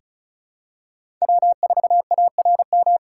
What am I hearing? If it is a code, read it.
Morse code, heard as a keyed tone, about 36 wpm, 700 Hz. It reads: W4ARM
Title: USA Call Signs - 5 Characters - 3x Character Spacing - 35wpm